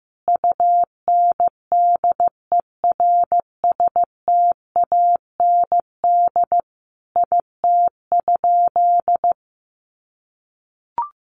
Morse code, heard as a keyed tone, about 15 wpm, 700 Hz. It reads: UNDERSTAND IT?
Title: Sentences from Top 600 Words 15wpm